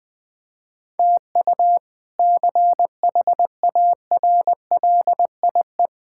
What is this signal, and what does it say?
Morse code, heard as a keyed tone, about 21 wpm, 700 Hz. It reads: TU CHARLIE
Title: K1USN SST QSO Practice - 20wpm